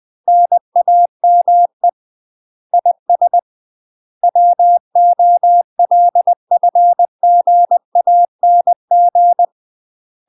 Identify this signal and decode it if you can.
Morse code, heard as a keyed tone, about 20 wpm, 700 Hz. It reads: NAME IS WOLFGANG
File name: UK MASTER QSOs TRAINING / 20WPM / UK MASTER QSOs 0040